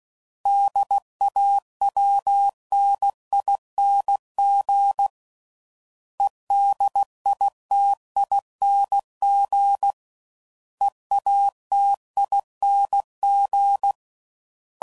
Morse code, heard as a keyed tone, about 16 wpm, 800 Hz. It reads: DAWNING EDITING EATING
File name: Suffixes / Special INGs / Sngl Ings